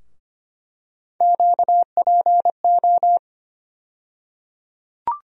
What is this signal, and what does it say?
Morse code, heard as a keyed tone, about 25 wpm, 700 Hz. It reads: QPO
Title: USA Call Sign Postfixes - 3 Letters 25wpm